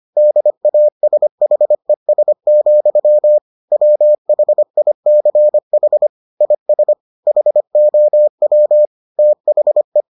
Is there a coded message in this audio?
Yes: Morse code from 0.2 to 10.0 s, heard as a keyed tone, about 25 wpm, 600 Hz. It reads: DASHES, WHICH IS HOW THE